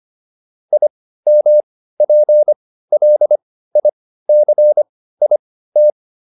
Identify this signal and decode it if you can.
Morse code, heard as a keyed tone, about 25 wpm, spaced as for 15 wpm, 600 Hz. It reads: IMPLICIT